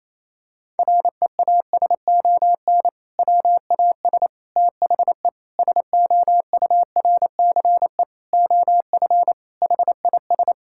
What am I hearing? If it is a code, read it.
Morse code, heard as a keyed tone, about 28 wpm, 700 Hz. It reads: REASON WAS THE SOURCE OF HIS
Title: Sentences from Top 1000 Words - Review Part 5 - 28wpm